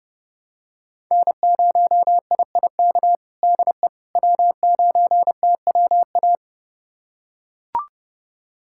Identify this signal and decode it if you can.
Morse code, heard as a keyed tone, about 32 wpm, 700 Hz. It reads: N0IIK DE W9TWA
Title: USA Callsign DE Callsign - 5 Characters - 30wpm